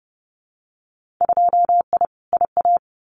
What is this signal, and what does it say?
Morse code, heard as a keyed tone, about 30 wpm, 700 Hz. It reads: 2I IA
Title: Winter Field Day Exchanges Encore 30wpm